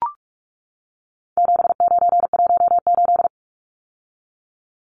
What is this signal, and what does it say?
Morse code, heard as a keyed tone, about 44 wpm, 700 Hz. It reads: 7918
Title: Numbers - 4-Digit 45wpm